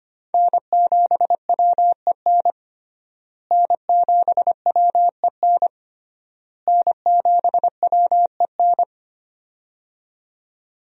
Morse code, heard as a keyed tone, about 25 wpm, 700 Hz. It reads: N7WEN N7WEN N7WEN
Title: USA Call Signs - 5 Characters - Repeated 3x - 25wpm